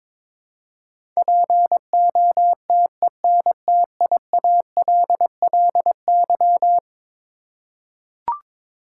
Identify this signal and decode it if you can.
Morse code, heard as a keyed tone, about 21 wpm, 700 Hz. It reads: POTENTIALLY
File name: practice / 22wpm / Eleven Letter Words - 22wpm